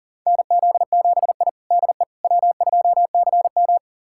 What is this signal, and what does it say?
Morse code, heard as a keyed tone, about 40 wpm, 700 Hz. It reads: NZ7I DE W2CM
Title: USA Callsign DE Callsign - 4 Characters - 40wpm